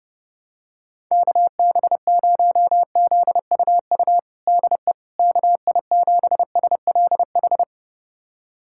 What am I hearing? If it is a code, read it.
Morse code, heard as a keyed tone, about 30 wpm, 700 Hz. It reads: KB0ZUU DE KI7SLH
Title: USA Callsign DE Callsign - 6 Characters - 30wpm